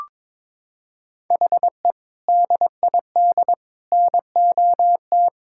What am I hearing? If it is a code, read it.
Morse code, heard as a keyed tone, about 22 wpm, 700 Hz. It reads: HE DID NOT
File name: practice / 22wpm / Sentences with 4-Word Phrases - 22wpm